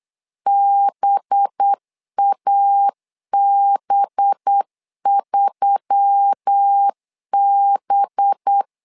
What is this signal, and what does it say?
Morse code, heard as a keyed tone, about 8 wpm, 800 Hz. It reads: BAB3B
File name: LEZIONI / 5 LBVC / 5 LB 8wpm 2816